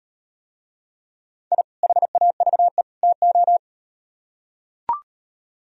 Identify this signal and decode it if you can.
Morse code, heard as a keyed tone, about 38 wpm, 700 Hz. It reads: I HAVE TO